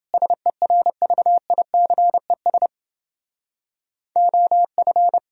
Morse code, heard as a keyed tone, about 30 wpm, 700 Hz. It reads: SERVICES OF